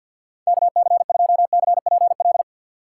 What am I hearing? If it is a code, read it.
Morse code, heard as a keyed tone, about 50 wpm, 700 Hz. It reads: KC2CPL
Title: USA Call Signs - 6 Characters - 6x Character Spacing - 50wpm